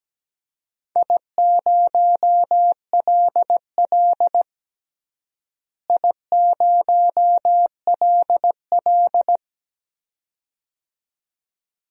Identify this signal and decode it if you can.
Morse code, heard as a keyed tone, about 17 wpm, 700 Hz. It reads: I0LL I0LL
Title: DXCC-100 Call Signs - 4 Characters - Repeated 2x - 17wpm